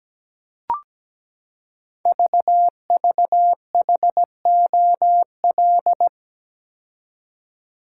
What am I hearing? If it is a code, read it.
Morse code, heard as a keyed tone, about 17 wpm, 700 Hz. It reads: VVHOL
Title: Code Groups - 5 Letters - 17wpm